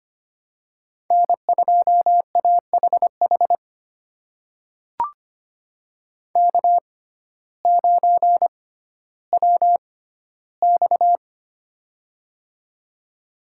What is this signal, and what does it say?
Morse code, heard as a keyed tone, about 25 wpm, spaced as for 14 wpm, 700 Hz. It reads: N2AHH K9WX
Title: USA Call Signs - 6x Character Spacing - 25wpm